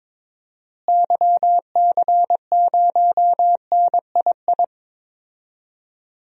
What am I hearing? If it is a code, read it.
Morse code, heard as a keyed tone, about 22 wpm, 700 Hz. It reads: YC0NII
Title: DXCC-100 Call Signs - 6 Characters - Repeated 2x - 22wpm